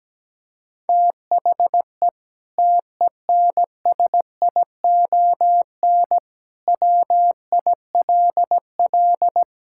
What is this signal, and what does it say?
Morse code, heard as a keyed tone, about 17 wpm, 700 Hz. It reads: THE TENSION WILL
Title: Sentences from Top 1600 Words 17wpm